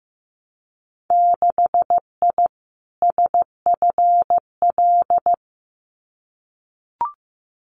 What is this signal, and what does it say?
Morse code, heard as a keyed tone, about 15 wpm, 700 Hz. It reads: 6I SFL
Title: Winter Field Day Exchanges 15wpm